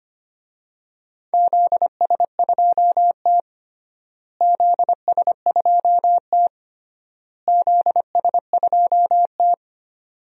Morse code, heard as a keyed tone, about 25 wpm, 700 Hz. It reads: ZS2T ZS2T ZS2T
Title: DXCC-100 Call Signs - 4 Characters - Repeated 3x - 25wpm